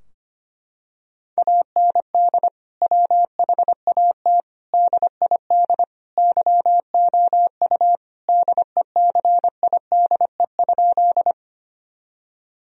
Morse code, heard as a keyed tone, about 25 wpm, 700 Hz. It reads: AND WHAT DID YOU DECIDE?
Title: Sentences from Top 1000 Words - Review Part 3 - 25wpm